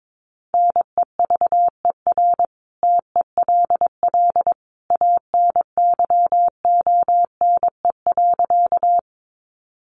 Morse code, heard as a keyed tone, about 22 wpm, 700 Hz. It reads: NEVER TELL ANYONE.
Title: Sentences from Top 900 Words 22wpm